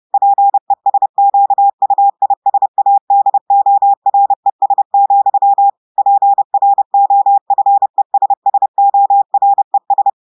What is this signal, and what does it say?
Morse code, heard as a keyed tone, about 30 wpm, 800 Hz. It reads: PESQUISADORES, PROFESSORES
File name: audios / portugais / portugais texte en clair 1 vitesse 30 mots minutes